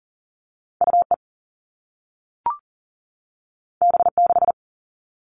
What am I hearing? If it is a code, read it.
Morse code, heard as a keyed tone, about 38 wpm, 700 Hz. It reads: UE B6
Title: All Cut Numbers - 2-Digit 40wpm